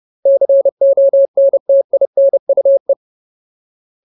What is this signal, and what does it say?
Morse code, heard as a keyed tone, about 30 wpm, 550 Hz. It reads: CONTINUE
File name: Ultimate Word List / 30WPM Ultimate